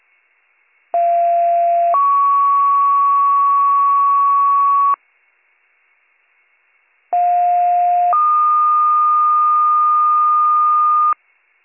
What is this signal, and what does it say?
Two-tone page, heard as a one-second tone, then a three-second tone, about 690 then 1120 Hz, about 710 then 1220 Hz. It is heard from 0.9 to 11.1 s.